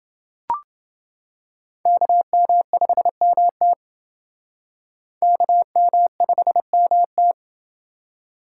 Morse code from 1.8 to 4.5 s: KM5MT

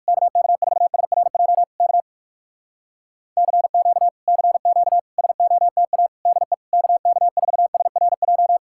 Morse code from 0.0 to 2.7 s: KK4SRJ K

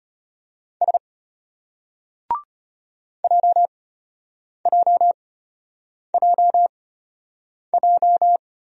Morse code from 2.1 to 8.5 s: JJJJ

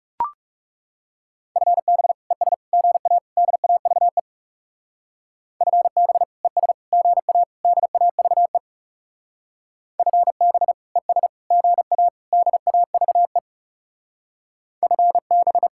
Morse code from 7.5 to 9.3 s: DAVE

